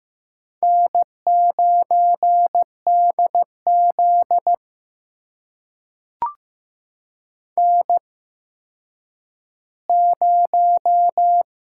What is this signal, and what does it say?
Morse code, heard as a keyed tone, about 15 wpm, spaced as for 9 wpm, 700 Hz. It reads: N9DZ N0
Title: USA Call Signs - 4 Characters - 8x Character Spacing - 15wpm